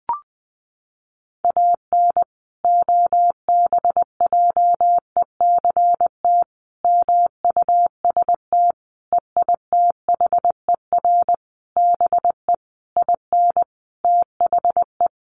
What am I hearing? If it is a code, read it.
Morse code, heard as a keyed tone, about 20 wpm, 700 Hz. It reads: AN OBJECT MUST EITHER BE IN THE
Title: Sentences from Top 1400 Words 20wpm